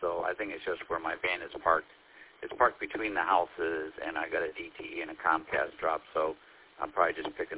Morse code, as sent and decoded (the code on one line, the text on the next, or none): none